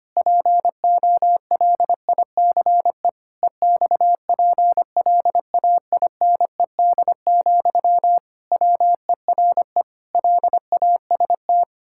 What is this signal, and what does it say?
Morse code, heard as a keyed tone, about 25 wpm, 700 Hz. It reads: POLICE EXPLAINED, WERE LAST
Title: Pangram Challenge v2 - 25wpm